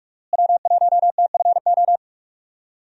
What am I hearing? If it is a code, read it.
Morse code, heard as a keyed tone, about 46 wpm, 700 Hz. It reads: W1TFO
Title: USA Call Signs - 5 Characters - Repeated 2x - 45wpm